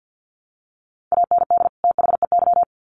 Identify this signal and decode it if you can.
Morse code, heard as a keyed tone, about 50 wpm, 700 Hz. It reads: AND THEY
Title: Sets of 2 Words 50wpm